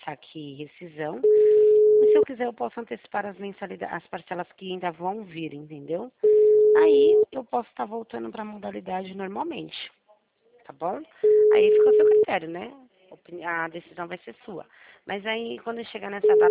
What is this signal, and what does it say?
Ringback tone (Europe, 425 Hz), with speech.
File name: out- -2009- - - .64